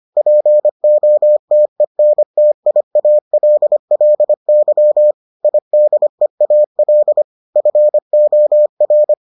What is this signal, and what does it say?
Morse code, heard as a keyed tone, about 25 wpm, 600 Hz. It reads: POTENTIALLY IDEAL FOR